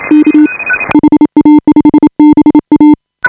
Morse code, heard as a keyed tone, about 30 wpm, 310 Hz. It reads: K HA5BA
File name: AUDIO2018 / HA5BA CW